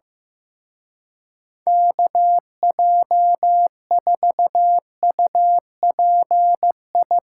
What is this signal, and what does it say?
Morse code, heard as a keyed tone, about 16 wpm, 700 Hz. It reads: KJ4UPI
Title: Speed Racing - USA 2x3 Call Signs 15wpm